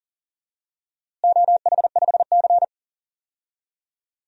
Morse code, heard as a keyed tone, about 40 wpm, 700 Hz. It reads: OH5C
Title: DXCC-100 Call Signs - 4 Characters - Repeated 3x - 40wpm